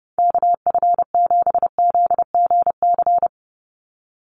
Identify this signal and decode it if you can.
Morse code, heard as a keyed tone, about 30 wpm, 700 Hz. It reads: KF7ZGC
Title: USA Call Signs 30wpm